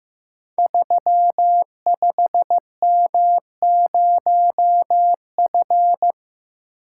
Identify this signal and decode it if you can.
Morse code, heard as a keyed tone, about 15 wpm, 700 Hz. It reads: 35M0F
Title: Code Groups - 5 Letter:Numbers - 15wpm